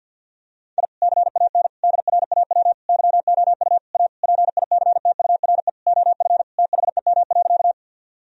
Morse code, heard as a keyed tone, about 52 wpm, 700 Hz. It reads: I CAN DRAW YOU A PICTURE OF THEM.
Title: Sentences from Top 1000 Words - Review Part 1 - 50wpm